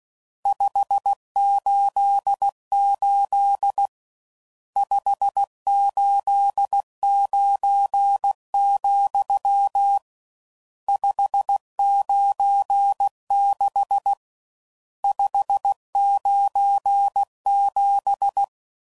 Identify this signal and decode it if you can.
Morse code, heard as a keyed tone, about 16 wpm, 800 Hz. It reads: 588 589, 596 597
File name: CW Code Groups @15WPM / RSTs / Various RST Nos